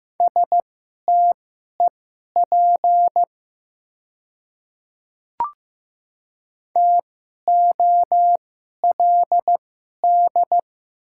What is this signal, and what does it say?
Morse code, heard as a keyed tone, about 15 wpm, spaced as for 11 wpm, 690 Hz. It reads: STEP TOLD
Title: 2x Character Spacing - Top 500 Words - 15wpm